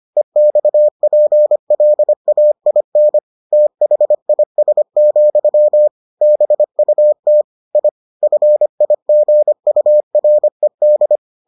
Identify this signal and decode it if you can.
Morse code, heard as a keyed tone, about 25 wpm, 600 Hz. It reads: EXPLAIN THIS, BUT I FIGURED